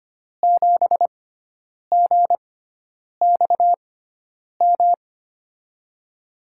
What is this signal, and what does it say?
Morse code, heard as a keyed tone, about 25 wpm, spaced as for 9 wpm, 700 Hz. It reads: 7GXM
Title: USA Call Signs - 6 Characters - 6x Character Spacing - 25wpm